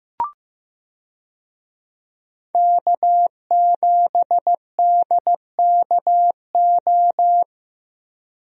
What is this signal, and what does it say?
Morse code, heard as a keyed tone, about 15 wpm, 700 Hz. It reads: K7DKO
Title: USA Call Signs - 5 Characters - Repeated 2x - 15wpm